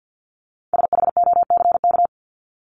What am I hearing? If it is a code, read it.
Morse code, heard as a keyed tone, about 48 wpm, 700 Hz. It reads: SHOCK
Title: Top 1751-2000 Words 50wpm